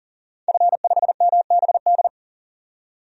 Morse code, heard as a keyed tone, about 37 wpm, 700 Hz. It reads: F5MBD
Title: DXCC-100 Call Signs - 5 Characters - 40wpm